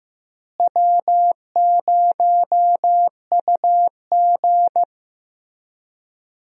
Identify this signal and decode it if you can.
Morse code, heard as a keyed tone, about 15 wpm, 700 Hz. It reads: W0UG
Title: USA Call Signs - 4 Characters - Repeated 3x - 15wpm